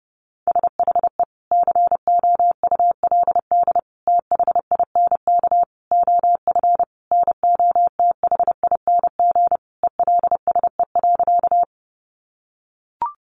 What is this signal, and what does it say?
Morse code, heard as a keyed tone, about 30 wpm, 700 Hz. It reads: SHE COULD THINK OF NOTHING ELSE.
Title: Sentences from Top 800 Words 30wpm